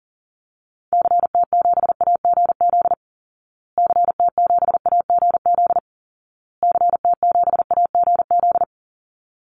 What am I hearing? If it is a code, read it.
Morse code, heard as a keyed tone, about 40 wpm, 700 Hz. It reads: CT7AGZ CT7AGZ CT7AGZ
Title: Call Signs Repeated 2x - 40wpm